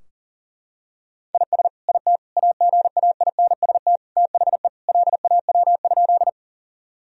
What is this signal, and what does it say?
Morse code, heard as a keyed tone, about 40 wpm, 700 Hz. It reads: IS IT AGAINST THE LAW?
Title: Sentences from Top 1000 Words - Review Part 3 - 40wpm